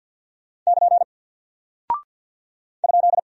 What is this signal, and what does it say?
Morse code, heard as a keyed tone, about 52 wpm, 700 Hz. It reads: <KN> ?
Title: Single Character (Letter:Number:Punctuation:Prosign) Alt 50wpm